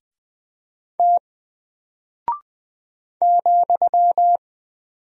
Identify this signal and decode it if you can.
Morse code, heard as a keyed tone, about 20 wpm, 700 Hz. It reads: T ,
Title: Single Character (Letter:Number:Punctuation:Prosign) Alt 20wpm